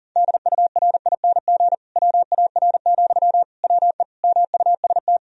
Morse code, heard as a keyed tone, about 40 wpm, 700 Hz. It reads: DURING WAR, WE MUST
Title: Sentences from Top 1000 Words - Review Part 4 - 40wpm